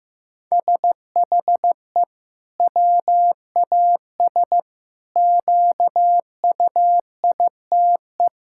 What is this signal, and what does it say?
Morse code, heard as a keyed tone, about 15 wpm, 700 Hz. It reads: SHE WAS QUITE
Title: Sentences from Top 1000 Words - Review Part 5 - 15wpm